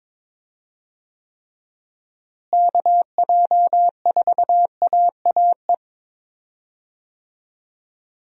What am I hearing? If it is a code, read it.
Morse code, heard as a keyed tone, about 22 wpm, 700 Hz. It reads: KJ4AAE
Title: USA Call Signs - 6 Characters - Repeated 2x - 22wpm